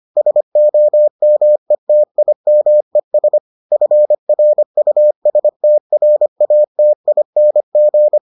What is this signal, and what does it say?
Morse code, heard as a keyed tone, about 25 wpm, 600 Hz. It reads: SOMETIMES FRUSTRATING